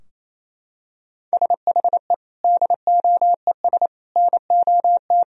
Morse code, heard as a keyed tone, about 28 wpm, 700 Hz. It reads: SHE DOES NOT
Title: Sentences from Top 1000 Words - Review Part 3 - 28wpm